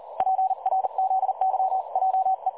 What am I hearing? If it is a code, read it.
Morse code, heard as a keyed tone, about 39 wpm, 760 Hz. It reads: PRZYJE
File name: dlazaawansowanych / nagraniazpasma / websdr recording 3565.0kHz